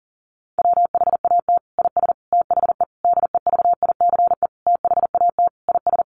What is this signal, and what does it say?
Morse code, heard as a keyed tone, about 37 wpm, 700 Hz. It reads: WHAT IS THE DEVICE THAT IS
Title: Sentences from Top 1500 Words 40wpm